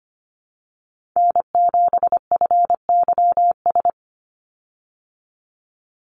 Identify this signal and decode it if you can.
Morse code, heard as a keyed tone, about 25 wpm, 700 Hz. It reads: N7FYS
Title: USA Call Signs 25wpm